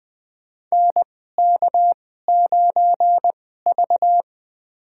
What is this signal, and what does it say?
Morse code, heard as a keyed tone, about 21 wpm, spaced as for 15 wpm, 700 Hz. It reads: NK9V
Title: USA Call Signs - 4 Characters - 2x Character Spacing - 20wpm